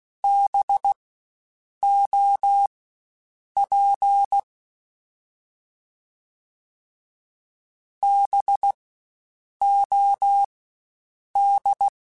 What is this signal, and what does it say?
Morse code, heard as a keyed tone, about 16 wpm, spaced as for 7 wpm, 800 Hz. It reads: BOP BOD
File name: Spaced Alphas / Spaced Alphas mp3s / Bs sp @15